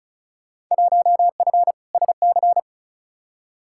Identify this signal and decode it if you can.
Morse code, heard as a keyed tone, about 36 wpm, 700 Hz. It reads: 1F SC
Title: Field Day Exchanges Encore - 35wpm